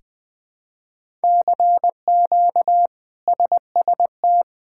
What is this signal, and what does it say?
Morse code, heard as a keyed tone, about 20 wpm, 700 Hz. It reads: CQ SST
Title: K1USN SST QSO Practice - 20wpm